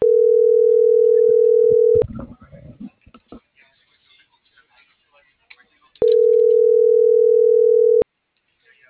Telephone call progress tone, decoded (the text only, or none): ringback tone